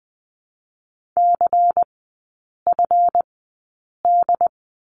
Morse code, heard as a keyed tone, about 20 wpm, spaced as for 8 wpm, 700 Hz. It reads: CFD
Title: 138 Lesson 23e - Letter f - Three Characters 20wpm